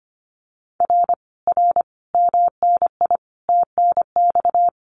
Morse code, heard as a keyed tone, about 25 wpm, 700 Hz. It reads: R R MNI TNX
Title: Common QSO Phrases Encore 25wpm